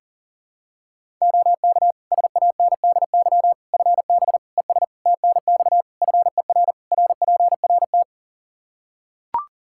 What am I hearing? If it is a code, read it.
Morse code, heard as a keyed tone, about 40 wpm, 700 Hz. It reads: OK SANDY FB ES TNX FER RPRT
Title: K1USN SST QSO Practice 40wpm